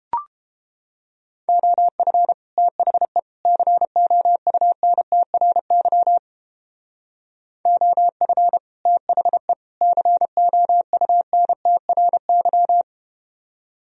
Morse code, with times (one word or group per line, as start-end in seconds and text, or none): 1.5-2.3 OF
2.6-3.2 THE
3.5-6.2 COUNTRY
7.7-8.6 OF
8.9-9.5 THE
9.8-12.8 COUNTRY